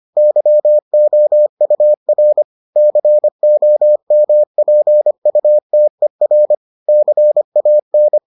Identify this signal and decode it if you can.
Morse code, heard as a keyed tone, about 25 wpm, 600 Hz. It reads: YOUR COMPUTER CAN